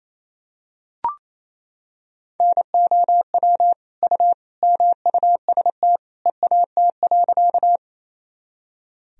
Morse code, heard as a keyed tone, about 28 wpm, 700 Hz. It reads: NOW U MUST EAT.
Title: Sentences from Top 300 Words and Abbreviations 28wpm